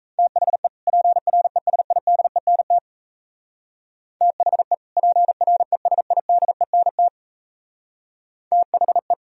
Morse code, heard as a keyed tone, about 36 wpm, 700 Hz. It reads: THE PRESIDENT THE PRESIDENT THE